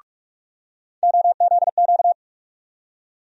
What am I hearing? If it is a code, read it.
Morse code, heard as a keyed tone, about 48 wpm, 700 Hz. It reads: OZQ
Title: USA Call Sign Postfixes - 3 Letters 45wpm